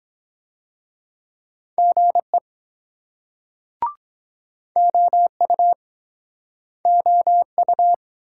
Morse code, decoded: GE OU OU

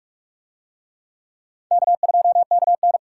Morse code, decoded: K2KN

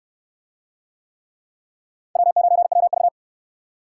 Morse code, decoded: W9WV

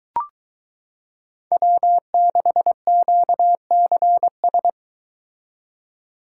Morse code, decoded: W6QCS

